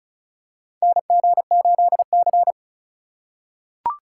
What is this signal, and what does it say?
Morse code, heard as a keyed tone, about 36 wpm, 700 Hz. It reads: NG8C
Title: USA Call Signs - 4 Characters - 4x Character Spacing - 35wpm